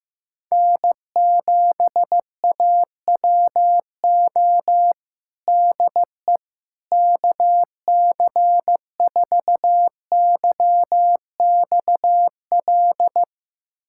Morse code, heard as a keyed tone, about 14 wpm, 700 Hz. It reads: N7AWO DE KC4YXL
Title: USA Callsign DE Callsign 15wpm